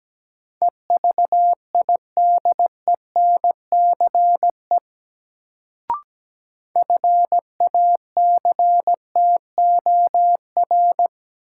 Morse code, heard as a keyed tone, about 17 wpm, 700 Hz. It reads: EVIDENCE FACTOR
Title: Sentences from Top 1300 Words - New Word Drill Alt - 17wpm